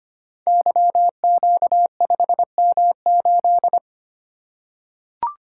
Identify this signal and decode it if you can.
Morse code, heard as a keyed tone, about 25 wpm, 700 Hz. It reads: YQ5M8
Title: Code Groups - 5 Letter:Numbers - 25wpm